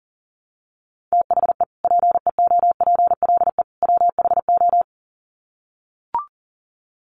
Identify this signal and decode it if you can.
Morse code, heard as a keyed tone, about 40 wpm, 700 Hz. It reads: THE PEOPLE WHO
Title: Sets of 3 Words 40wpm